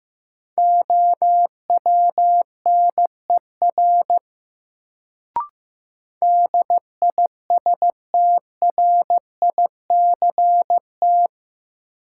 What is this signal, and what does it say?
Morse code, heard as a keyed tone, about 15 wpm, 700 Hz. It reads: OWNER DISTRICT
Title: Sentences from Top 1600 Words - New Word Drill - 15wpm